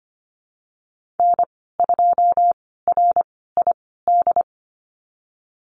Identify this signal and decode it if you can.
Morse code, heard as a keyed tone, about 25 wpm, spaced as for 16 wpm, 700 Hz. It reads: N2RID
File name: Lesson 15g - Number 5 - Call Signs 25wpm